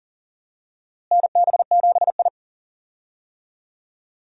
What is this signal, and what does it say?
Morse code, heard as a keyed tone, about 40 wpm, 700 Hz. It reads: NB7I
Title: USA Call Signs - 4 Characters - Repeated 3x - 40wpm